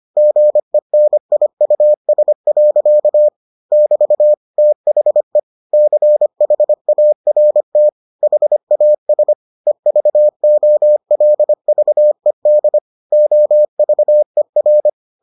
Morse code, heard as a keyed tone, about 25 wpm, 600 Hz. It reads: GENIUS. <BT> THE CHART HAS EVOLVED OVER